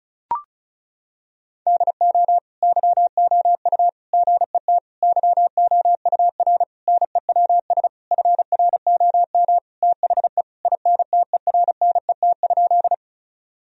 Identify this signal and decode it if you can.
Morse code, heard as a keyed tone, about 36 wpm, 700 Hz. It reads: DO YOU GET YOUR NEWS FROM THE INTERNET?
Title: Sentences from Top 1000 Words - Review Part 4 - 35wpm